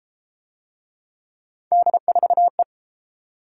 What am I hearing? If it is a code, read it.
Morse code, heard as a keyed tone, about 33 wpm, 700 Hz. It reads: D4E